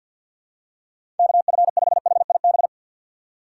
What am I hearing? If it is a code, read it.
Morse code, heard as a keyed tone, about 53 wpm, 700 Hz. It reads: KF5HIB
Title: USA Call Signs - 6 Characters - 6x Character Spacing - 50wpm